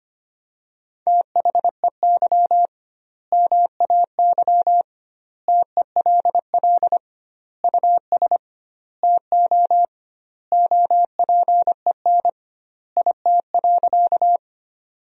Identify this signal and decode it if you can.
Morse code, heard as a keyed tone, about 25 wpm, 700 Hz. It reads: THEY MAY TELL US TO OPEN IT.
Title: 2x Word-Spacing - Sentences from Top 300 Words - 25wpm